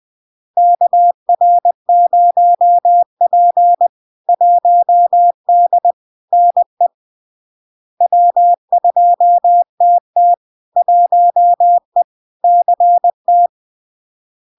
Morse code, heard as a keyed tone, about 20 wpm, 700 Hz. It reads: KR0P 1D NE W2TT 1E CT
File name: LARK-FD-2022-Log-in-CW-20wpm-700-Hz